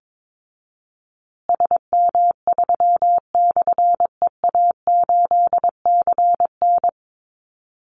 Morse code, heard as a keyed tone, about 22 wpm, 700 Hz. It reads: SM3/EA8CN
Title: Call Signs Repeated 3x - 22wpm